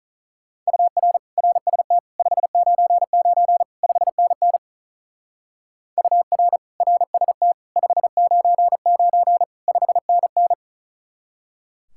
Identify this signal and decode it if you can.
Morse code, heard as a keyed tone, about 43 wpm, 700 Hz. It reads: UR RST 599 5NN UR RST 599 5NN
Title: Speed-Racing Common QSO Phrases Encore - 35wpm